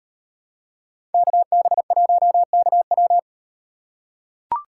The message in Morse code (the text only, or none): KB1KW